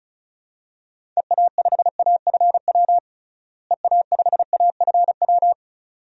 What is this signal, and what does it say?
Morse code, heard as a keyed tone, about 34 wpm, 700 Hz. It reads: EA5AFW EA5AFW
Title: DXCC-100 Call Signs - 6 Characters - Repeated 2x - 35wpm